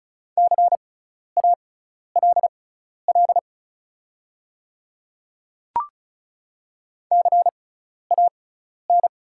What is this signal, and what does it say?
Morse code, heard as a keyed tone, about 34 wpm, spaced as for 12 wpm, 700 Hz. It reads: CALL CAN
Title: 6x Character Spacing - Top 200 Words - 35wpm